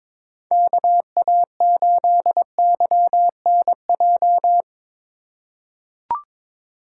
Morse code, heard as a keyed tone, about 22 wpm, 700 Hz. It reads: KA8YNJ
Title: USA Call Signs - 6 Characters - 2x Character Spacing - 22wpm